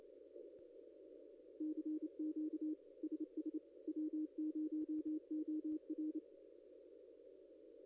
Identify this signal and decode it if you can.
Morse code, heard as a keyed tone, about 29 wpm, 310 Hz. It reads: CQ SS W0OR
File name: contests / ss-cw-k5tr-op-n5rz -0900